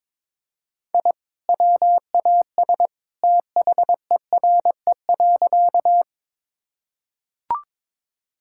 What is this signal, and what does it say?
Morse code, heard as a keyed tone, about 22 wpm, 700 Hz. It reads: I WAS THERE.
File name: practice / 22wpm / Sentences with 2-Word Phrases - 22wpm